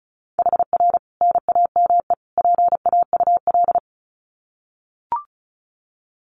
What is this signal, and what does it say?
Morse code, heard as a keyed tone, about 34 wpm, 700 Hz. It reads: HR NAME PAUL
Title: Common QSO Phrases 35wpm